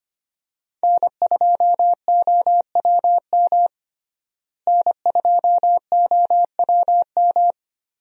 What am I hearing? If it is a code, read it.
Morse code, heard as a keyed tone, about 25 wpm, 700 Hz. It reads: N2OWM N2OWM
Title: USA Call Signs - 5 Characters - Repeated 3x - 25wpm